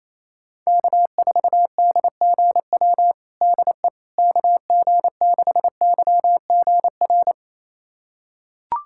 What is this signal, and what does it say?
Morse code, heard as a keyed tone, about 28 wpm, 700 Hz. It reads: K4DGW DE KG6YGR
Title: USA Callsign DE Callsign 28wpm